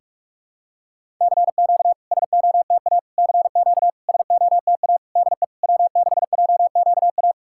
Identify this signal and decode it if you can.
Morse code, heard as a keyed tone, about 48 wpm, 700 Hz. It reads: CQ SOTA CQ SOTA DE W6JQA
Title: SOTA QSO Practice 45wpm